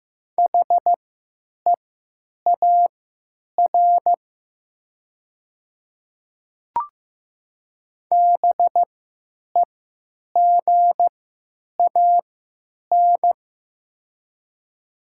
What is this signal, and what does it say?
Morse code, heard as a keyed tone, about 15 wpm, spaced as for 9 wpm, 700 Hz. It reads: HEAR BEGAN
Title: 3x Character Spacing - Top 300 Words - 15wpm